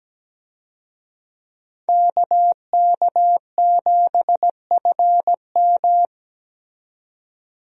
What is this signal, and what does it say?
Morse code, heard as a keyed tone, about 17 wpm, 700 Hz. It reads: KK7FM
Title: USA Call Signs - 5 Characters - 17wpm